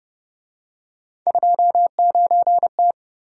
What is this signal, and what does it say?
Morse code, heard as a keyed tone, about 29 wpm, 700 Hz. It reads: 29T